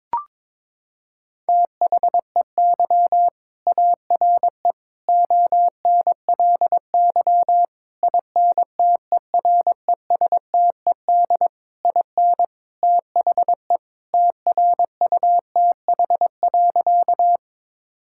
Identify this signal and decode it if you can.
Morse code, heard as a keyed tone, about 21 wpm, 700 Hz. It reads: THEY ARE ONLY INTERESTED IN THE TRUTH.
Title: Sentences from Top 1000 Words - Review Part 6 - 22wpm